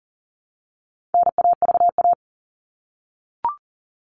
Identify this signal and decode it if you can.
Morse code, heard as a keyed tone, about 40 wpm, 700 Hz. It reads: NAVA
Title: All Cut Numbers - 4-Digit 40wpm